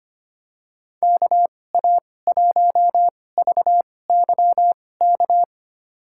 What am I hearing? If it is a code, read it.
Morse code, heard as a keyed tone, about 25 wpm, spaced as for 18 wpm, 700 Hz. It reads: KA1VYK
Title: USA Call Signs - 2x Character Spacing - 25wpm